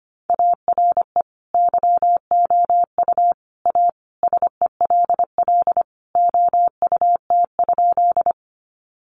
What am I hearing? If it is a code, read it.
Morse code, heard as a keyed tone, about 25 wpm, 700 Hz. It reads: ARE YOU A SELL OUT?
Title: Sentences from Top 1100 Words 25wpm